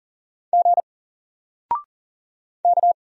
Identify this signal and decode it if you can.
Morse code, heard as a keyed tone, about 40 wpm, 700 Hz. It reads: G K
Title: Single Character (Letter:Number:Punctuation:Prosign) 40wpm